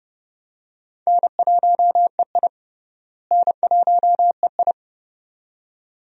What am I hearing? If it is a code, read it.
Morse code, heard as a keyed tone, about 30 wpm, 700 Hz. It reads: N1EI N1EI
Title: USA Call Signs - 4 Characters - Repeated 2x - 30wpm